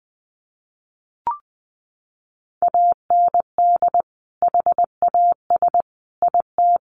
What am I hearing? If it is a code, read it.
Morse code, heard as a keyed tone, about 20 wpm, 700 Hz. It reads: AND HAS IT
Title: Sentences from Top 100 Words 20wpm